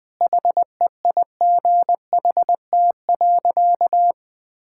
Morse code, heard as a keyed tone, about 20 wpm, 700 Hz. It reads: HEIGHT.
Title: Sentences from Top 1600 Words 20wpm